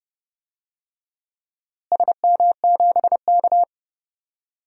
Morse code, heard as a keyed tone, about 30 wpm, 700 Hz. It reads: SM7K
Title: DXCC-100 Call Signs - 4 Characters - Repeated 2x - 30wpm